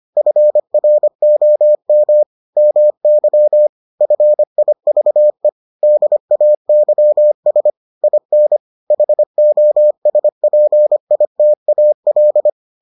Morse code, heard as a keyed tone, about 25 wpm, 600 Hz. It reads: FROM MY FIVE DAYS IN HOSPITAL